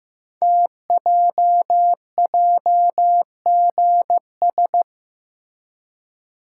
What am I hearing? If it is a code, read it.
Morse code, heard as a keyed tone, about 15 wpm, 700 Hz. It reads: TJJGS
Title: Code Groups - 5 Letters - 15wpm